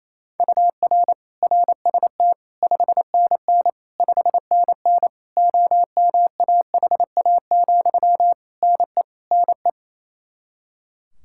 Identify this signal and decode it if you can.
Morse code, heard as a keyed tone, about 27 wpm, 700 Hz. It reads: UR RST 5NN 5NN OMAHA, NE NE
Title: Practice QSO - Exchange 1 - Beginning - 28wpm